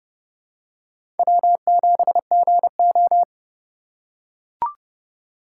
Morse code, heard as a keyed tone, about 30 wpm, 700 Hz. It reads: W7GO
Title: USA Call Signs - 4 Characters - 4x Character Spacing - 30wpm